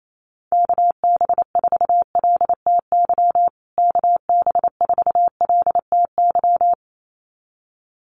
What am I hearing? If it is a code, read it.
Morse code, heard as a keyed tone, about 28 wpm, 700 Hz. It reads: KB4LTY KB4LTY
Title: USA Call Signs Repeated 2x 28wpm